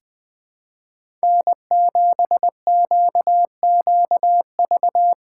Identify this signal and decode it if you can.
Morse code, heard as a keyed tone, about 21 wpm, 700 Hz. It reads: N7QQV
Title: USA Call Signs - 5 Characters - 4x Character Spacing - 20wpm